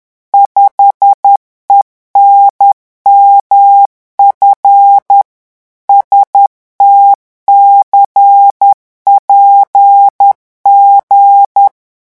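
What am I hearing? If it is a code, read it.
Morse code, heard as a keyed tone, about 10 wpm, 800 Hz. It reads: 5ENMF STCPG
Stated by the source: Morse revision exercise for French radio amateurs at 10 words per minute.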